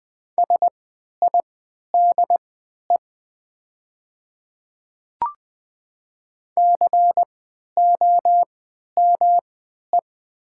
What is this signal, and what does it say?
Morse code, heard as a keyed tone, about 20 wpm, spaced as for 11 wpm, 700 Hz. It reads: SIDE COME
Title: 3x Character Spacing - Top 100 Words - 20wpm